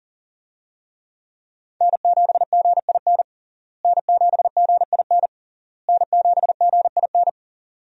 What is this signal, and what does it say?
Morse code, heard as a keyed tone, about 40 wpm, 700 Hz. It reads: N7GIN N7GIN N7GIN
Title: USA Call Signs - 5 Characters - Repeated 3x - 40wpm